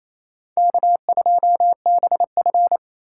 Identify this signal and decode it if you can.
Morse code, heard as a keyed tone, about 28 wpm, 700 Hz. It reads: K2BF
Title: USA Call Signs - 3x Character Spacing - 28wpm